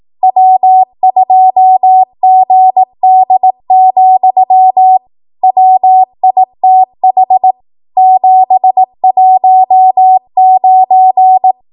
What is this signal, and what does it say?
Morse code, heard as a keyed tone, about 18 wpm, 750 Hz. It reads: W2GD, WITH 719